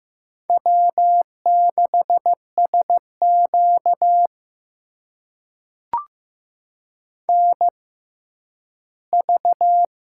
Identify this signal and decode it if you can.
Morse code, heard as a keyed tone, about 15 wpm, spaced as for 10 wpm, 700 Hz. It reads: W6SQ NV
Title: USA Call Signs - 4 Characters - 6x Character Spacing - 15wpm